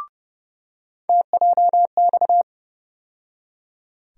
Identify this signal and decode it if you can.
Morse code, heard as a keyed tone, about 30 wpm, 700 Hz. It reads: TJX
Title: USA Call Sign Postfixes - 3 Letters 30wpm